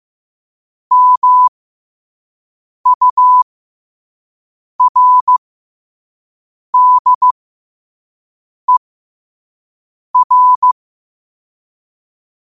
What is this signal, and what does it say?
Morse code, heard as a keyed tone, about 15 wpm, spaced as for 5 wpm, 1000 Hz. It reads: MURDER
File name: qotdfiles / 5wpm